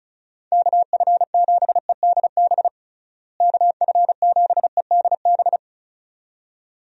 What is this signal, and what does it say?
Morse code, heard as a keyed tone, about 34 wpm, 700 Hz. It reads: KF7EDB KF7EDB
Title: USA Call Signs - 6 Characters - Repeated 2x - 35wpm